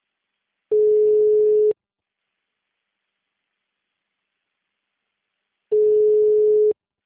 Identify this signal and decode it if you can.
Ringback tone (Europe, 425 Hz).
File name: out- -2007- - - .2883